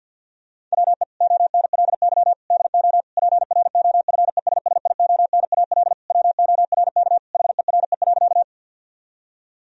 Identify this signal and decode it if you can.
Morse code, heard as a keyed tone, about 50 wpm, 700 Hz. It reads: WE ONLY DO PROFESSIONAL WORK HERE.
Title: Sentences from Top 1000 Words - Review Part 4 - 50wpm